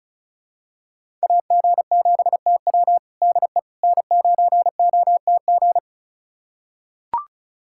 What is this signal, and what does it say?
Morse code, heard as a keyed tone, about 36 wpm, 700 Hz. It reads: AG7TW DE N9OTG
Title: USA Callsign DE Callsign - 5 Characters - 35wpm